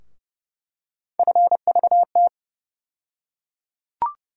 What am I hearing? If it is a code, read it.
Morse code, heard as a keyed tone, about 30 wpm, 700 Hz. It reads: FVT
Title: USA Call Sign Postfixes - 3 Letters 30wpm